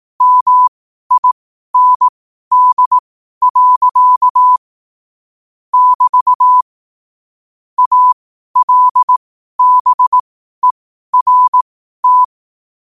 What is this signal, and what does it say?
Morse code, heard as a keyed tone, about 18 wpm, spaced as for 12 wpm, 1000 Hz. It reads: MIND. <BT> ALBERT
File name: 12wpm